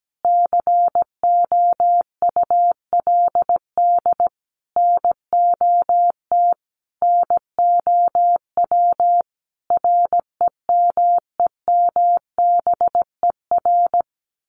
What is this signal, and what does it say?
Morse code, heard as a keyed tone, about 17 wpm, 700 Hz. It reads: COULD NOT NOW REMEMBER
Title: Sentences from Top 1200 Words 17wpm